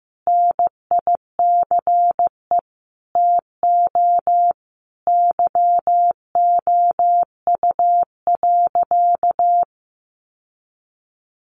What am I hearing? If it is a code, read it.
Morse code, heard as a keyed tone, about 15 wpm, 700 Hz. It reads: NICE TO YOU.
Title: Sentences from Top 1100 Words 15wpm